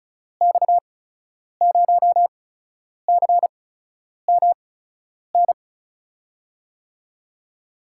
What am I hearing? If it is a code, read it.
Morse code, heard as a keyed tone, about 36 wpm, spaced as for 10 wpm, 700 Hz. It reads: X0CMN
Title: USA Call Signs - 6 Characters - 8x Character Spacing - 35wpm